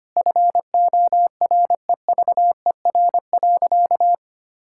Morse code, heard as a keyed tone, about 25 wpm, 700 Hz. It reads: FOREVER.
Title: Sentences from Top 1600 Words 25wpm